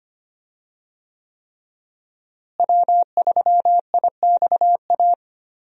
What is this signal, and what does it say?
Morse code, heard as a keyed tone, about 25 wpm, 700 Hz. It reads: W3IXA
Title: USA Call Signs - 5 Characters - Repeated 3x - 25wpm